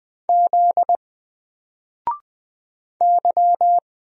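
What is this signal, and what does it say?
Morse code, heard as a keyed tone, about 20 wpm, 700 Hz. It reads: Z Y
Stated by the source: Single letter and number Morse drill at 20 wpm, from N4LCD.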